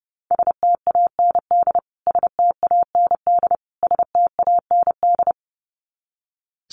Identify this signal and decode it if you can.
Morse code, heard as a keyed tone, about 30 wpm, 700 Hz. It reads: STAND STAND STAND
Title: Top 500 Words Repeated 30wpm